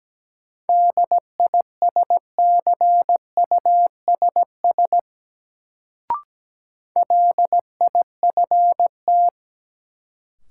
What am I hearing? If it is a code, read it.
Morse code, heard as a keyed tone, about 17 wpm, 700 Hz. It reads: DISCUSS LIFT
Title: Sentences from Top 1000 Words - New Word Drill - 17wpm